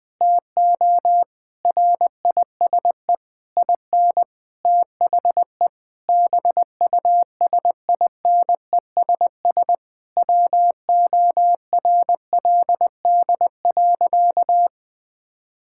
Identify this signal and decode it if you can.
Morse code, heard as a keyed tone, about 20 wpm, 700 Hz. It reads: TO RISE IN THE BUSINESS WORLD.
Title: Sentences from Top 1000 Words - Review Part 6 - 20wpm